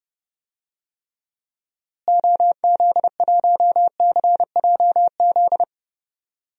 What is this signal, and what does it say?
Morse code, heard as a keyed tone, about 30 wpm, 700 Hz. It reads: OZ1CJZ
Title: DXCC-100 Call Signs - 6 Characters - Repeated 3x - 30wpm